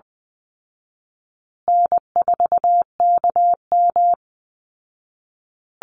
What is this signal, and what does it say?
Morse code, heard as a keyed tone, about 20 wpm, 700 Hz. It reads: N4KM
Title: Call Signs Repeated 2x - 20wpm